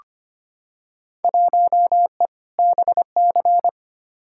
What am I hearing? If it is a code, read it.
Morse code, heard as a keyed tone, about 25 wpm, 700 Hz. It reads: 1E BC
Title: Field Day Exchanges - 25wpm